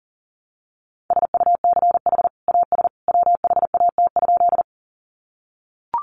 Morse code, heard as a keyed tone, about 40 wpm, 700 Hz. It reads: SUCH AS WHAT?
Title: Sentences from Top 200 Words 40wpm